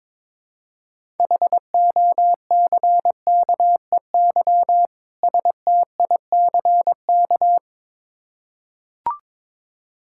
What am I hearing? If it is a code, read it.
Morse code, heard as a keyed tone, about 22 wpm, 700 Hz. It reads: HOCKEY STICK
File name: Difficult Words with Hints - 22wpm